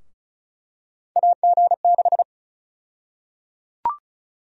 Morse code, heard as a keyed tone, about 34 wpm, 700 Hz. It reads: AG6
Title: USA Call Sign Prefixes - 35wpm